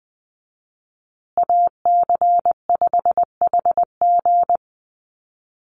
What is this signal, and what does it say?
Morse code, heard as a keyed tone, about 20 wpm, 700 Hz. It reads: AC5HG
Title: USA Call Signs Repeated 2x 20wpm